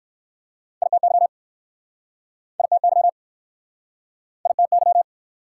Morse code, heard as a keyed tone, about 60 wpm, 690 Hz. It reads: ITY ITY ITY